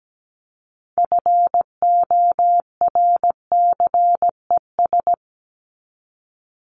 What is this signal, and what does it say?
Morse code, heard as a keyed tone, about 17 wpm, 700 Hz. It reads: FORCES
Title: ES:S Suffix - Top 100 Words with Base Words 17wpm